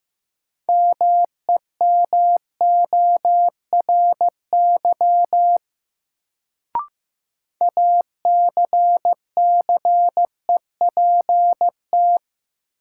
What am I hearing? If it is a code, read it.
Morse code, heard as a keyed tone, about 15 wpm, 700 Hz. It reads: MEMORY ACCEPT
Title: Sentences from Top 1300 Words - New Word Drill - 15wpm